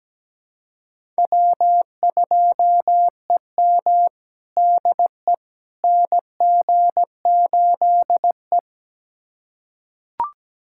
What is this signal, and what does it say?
Morse code, heard as a keyed tone, about 17 wpm, 700 Hz. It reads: W2EM DE NG8E
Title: USA Callsign DE Callsign - 4 Characters - 17wpm